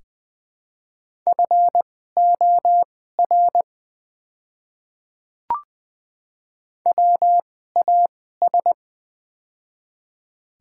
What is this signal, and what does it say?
Morse code, heard as a keyed tone, about 20 wpm, spaced as for 14 wpm, 700 Hz. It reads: FOR WAS
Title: 2x Character Spacing - Top 100 Words - 20wpm